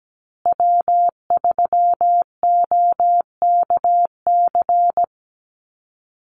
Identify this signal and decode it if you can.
Morse code, heard as a keyed tone, about 17 wpm, 700 Hz. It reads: W3OKC
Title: Call Signs Repeated 2x - 17wpm